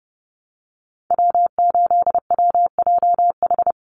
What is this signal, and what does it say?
Morse code, heard as a keyed tone, about 30 wpm, 700 Hz. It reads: W8WJH
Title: USA Call Signs 30wpm